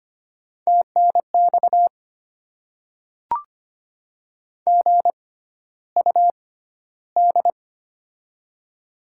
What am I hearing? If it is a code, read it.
Morse code, heard as a keyed tone, about 25 wpm, spaced as for 13 wpm, 690 Hz. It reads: TNX GUD
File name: Top 20 QSO Elements - 6x Character Spacing - 25wpm